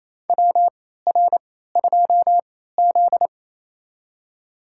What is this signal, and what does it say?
Morse code, heard as a keyed tone, about 28 wpm, spaced as for 16 wpm, 700 Hz. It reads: WR2Z
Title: USA Call Signs - 3x Character Spacing - 28wpm